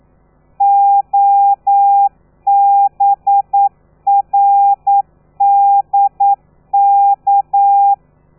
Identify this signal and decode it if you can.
Morse code, heard as a keyed tone, about 9 wpm, 800 Hz. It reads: OBRDK
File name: telegrafia / i2tfj / Lez13 40